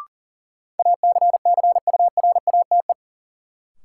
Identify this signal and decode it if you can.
Morse code, heard as a keyed tone, about 40 wpm, 700 Hz. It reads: ACCURATE
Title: Sentences from Top 1200 Words - New Word Drill - 40wpm